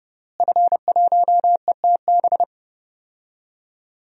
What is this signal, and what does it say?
Morse code, heard as a keyed tone, about 32 wpm, 700 Hz. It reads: F1ETB
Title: DXCC-100 Call Signs - 5 Characters - Repeated 3x - 30wpm